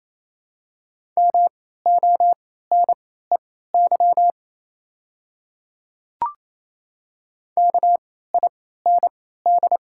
Morse code, heard as a keyed tone, about 28 wpm, spaced as for 16 wpm, 690 Hz. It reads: MONEY KIND
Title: 3x Character Spacing - Top 500 Words - 28wpm